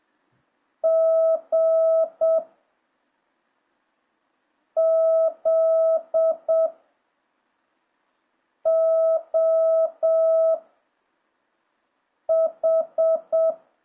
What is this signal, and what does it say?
Morse code, heard as a keyed tone, about 7 wpm, 650 Hz. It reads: G Z O H